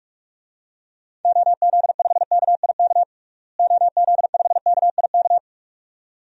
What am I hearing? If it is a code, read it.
Morse code, heard as a keyed tone, about 44 wpm, 700 Hz. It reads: OZ5KIK OZ5KIK
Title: DXCC-100 Call Signs - 6 Characters - Repeated 2x - 45wpm